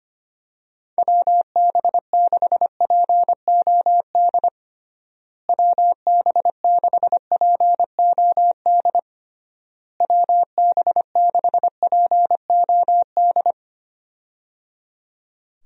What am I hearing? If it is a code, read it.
Morse code, heard as a keyed tone, about 25 wpm, 700 Hz. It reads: WB6POD WB6POD WB6POD
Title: USA Call Signs - 6 Characters - Repeated 3x - 25wpm